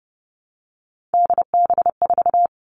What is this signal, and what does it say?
Morse code, heard as a keyed tone, about 30 wpm, 700 Hz. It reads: DB4
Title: Three Challenging Letter-Number Combinations 30wpm